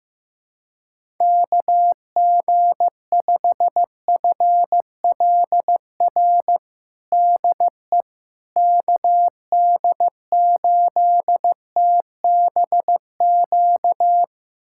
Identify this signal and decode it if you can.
Morse code, heard as a keyed tone, about 15 wpm, 700 Hz. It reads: KG5FLR DE KD8TBQ
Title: USA Callsign DE Callsign 15wpm